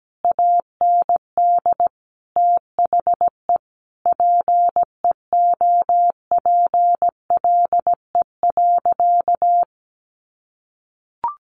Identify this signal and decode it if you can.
Morse code, heard as a keyed tone, about 17 wpm, 700 Hz. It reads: AND THE PEOPLE.
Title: Sentences from Top 1500 Words 17wpm